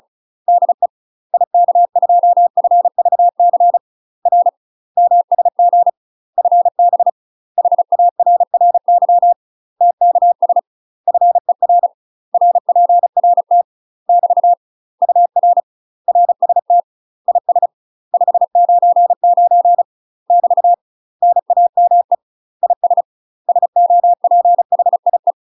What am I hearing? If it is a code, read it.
Morse code, heard as a keyed tone, about 36 wpm, 700 Hz. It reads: DE IK2FVC R MSG FB HARRY TKS FER RPRT <BT> UR RST IS 599 <BT> NAME IS SOPHIE
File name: UK MASTER QSOs TRAINING / 35WPM / UK MASTER QSOs 0032